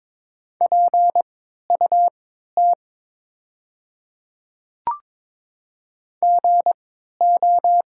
Morse code, heard as a keyed tone, about 23 wpm, spaced as for 13 wpm, 700 Hz. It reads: PUT GO
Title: 3x Character Spacing - Top 200 Words - 22wpm